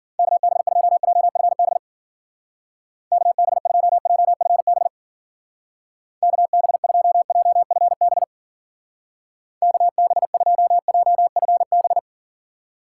Morse code, heard as a keyed tone, about 50 wpm, 700 Hz. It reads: KB2JFB KB2JFB KB2JFB KB2JFB
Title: Speed Racing - USA 2x3 Call Signs 40wpm